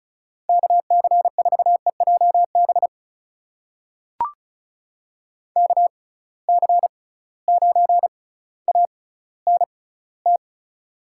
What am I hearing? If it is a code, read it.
Morse code, heard as a keyed tone, about 36 wpm, spaced as for 18 wpm, 700 Hz. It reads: KC4EJB KC9ANT
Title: USA Call Signs - 6x Character Spacing - 35wpm